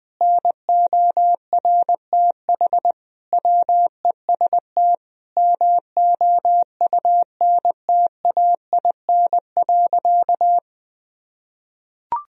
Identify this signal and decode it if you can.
Morse code, heard as a keyed tone, about 20 wpm, 700 Hz. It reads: NORTH WEST MOUNTAIN.
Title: Sentences from Top 1000 Words - Review Part 2 - 20wpm